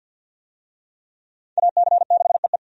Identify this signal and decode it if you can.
Morse code, heard as a keyed tone, about 48 wpm, 700 Hz. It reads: AC6EE
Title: USA Call Signs - 5 Characters - 50wpm